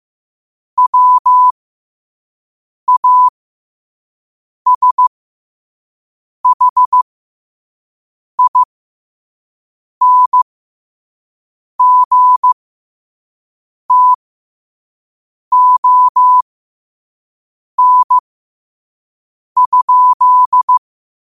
Morse code, heard as a keyed tone, about 15 wpm, spaced as for 5 wpm, 1000 Hz. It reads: WASHINGTON?